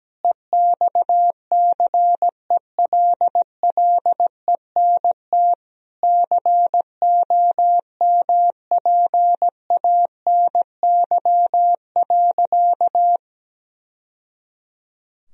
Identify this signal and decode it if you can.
Morse code, heard as a keyed tone, about 17 wpm, 700 Hz. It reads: EXCELLENT COMPANY.
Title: Sentences from Top 1000 Words - Review Part 3 - 17wpm